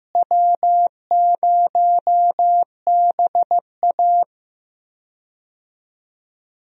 Morse code, heard as a keyed tone, about 15 wpm, 700 Hz. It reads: W0BA
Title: USA Call Signs - 4 Characters - Repeated 2x - 15wpm